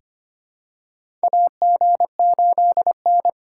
Morse code, heard as a keyed tone, about 25 wpm, 700 Hz. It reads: AG8N
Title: USA Call Signs - 4 Characters - 4x Character Spacing - 25wpm